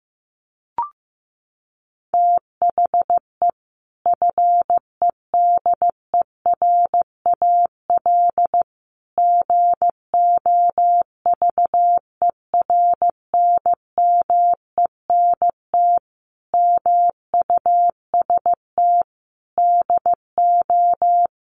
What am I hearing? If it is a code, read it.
Morse code, heard as a keyed tone, about 15 wpm, 700 Hz. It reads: THE FEDERAL GOVERNMENT MUST DO
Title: Sentences from Top 1100 Words 15wpm